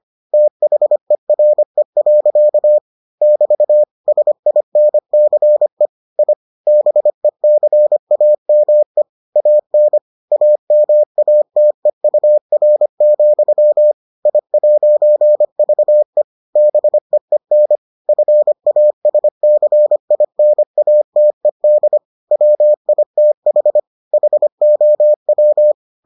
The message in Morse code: THERE. <BT> SINCE I BECAME AN AMATEUR, I'VE BEEN FASCINATED WITH HOW

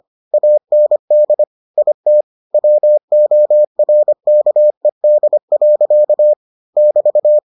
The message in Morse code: AND IT WORKED. <BT>

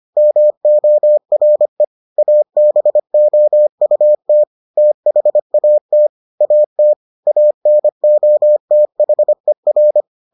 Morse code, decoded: MORE ABOUT THAT AT ANOTHER

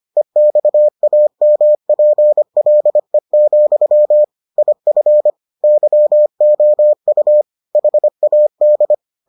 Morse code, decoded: EXAMPLE, IF YOU HAD

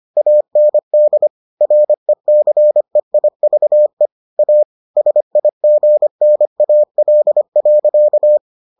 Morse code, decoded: AND RECEIVE A SIGNAL.

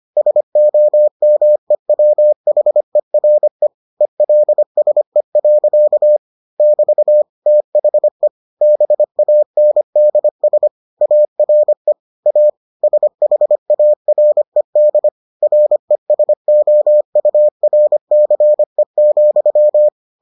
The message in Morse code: SOMEWHERE ELSE. <BT> THE BANDS ARE A SHARED RESOURCE,

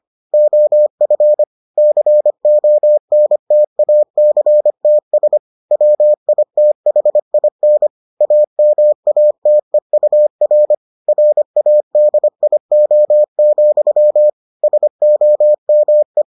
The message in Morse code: OF CONTACTS WITHIN AMATEUR RADIO, SOME

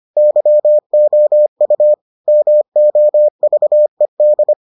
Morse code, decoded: YOU MOVED

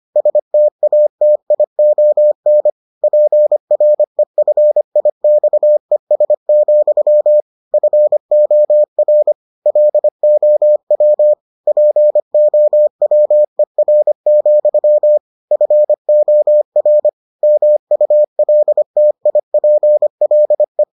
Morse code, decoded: STATION PREFIXES, FOR LOW POWER, FOR MULTIPLE

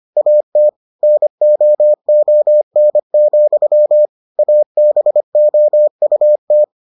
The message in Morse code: AT NOON, ABOUT

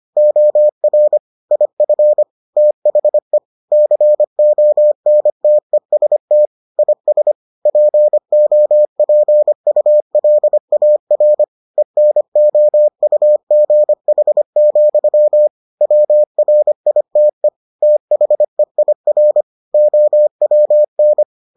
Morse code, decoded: OR IF THE CONTEST IS POPULAR ENOUGH, WRITE THEIR OWN

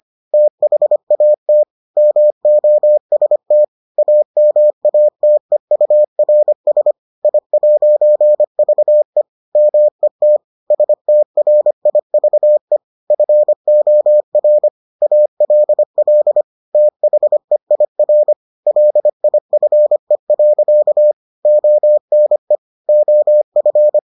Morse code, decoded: THAT MOST AMATEURS I'VE MET STRIVE FOR ALL THEIR LIFE. ONE OF

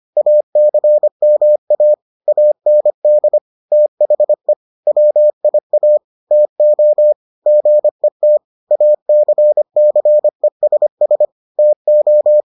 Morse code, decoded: ACMA AND THE WIA TO GET ACCESS TO

